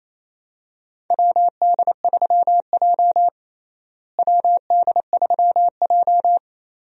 Morse code, heard as a keyed tone, about 27 wpm, 700 Hz. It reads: WD3J WD3J
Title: USA Call Signs - 4 Characters - Repeated 2x - 28wpm